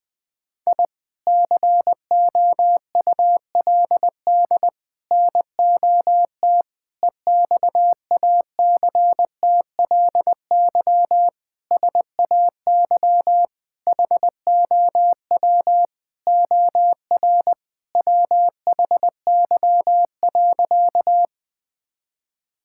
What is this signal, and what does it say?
Morse code, heard as a keyed tone, about 20 wpm, 700 Hz. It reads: I COULD NOT EXACTLY SAY HOW OR WHY.
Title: Sentences from Top 1000 Words - Review Part 6 - 20wpm